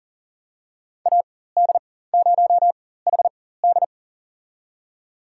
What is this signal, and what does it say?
Morse code, heard as a keyed tone, about 40 wpm, spaced as for 19 wpm, 700 Hz. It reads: AD0HD
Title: USA Call Signs - 5 Characters - 4x Character Spacing - 40wpm